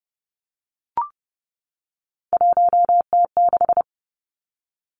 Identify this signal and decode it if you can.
Morse code, heard as a keyed tone, about 30 wpm, 700 Hz. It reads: 1T6
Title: Three Challenging Letter-Number Combinations 30wpm